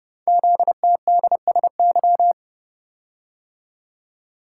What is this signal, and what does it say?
Morse code, heard as a keyed tone, about 29 wpm, 700 Hz. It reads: ZTDSY
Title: Code Groups - 5 Letters - 30wpm